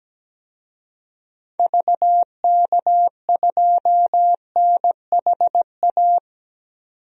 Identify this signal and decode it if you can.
Morse code, heard as a keyed tone, about 17 wpm, 700 Hz. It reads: VK2NHA
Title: DXCC-100 Call Signs - 6 Characters - 17wpm